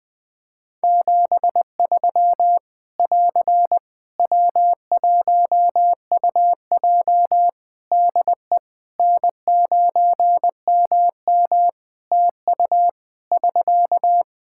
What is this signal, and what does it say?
Morse code, heard as a keyed tone, about 20 wpm, 700 Hz. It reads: 73 <AR> W1UJ DE N9MM TU <SK>
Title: Practice QSO - Exchange 4 - Ending - 20wpm